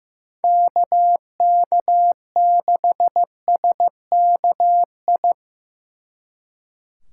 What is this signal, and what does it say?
Morse code, heard as a keyed tone, about 17 wpm, 700 Hz. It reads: KK6SKI
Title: Speed Racing - USA Call Signs 15wpm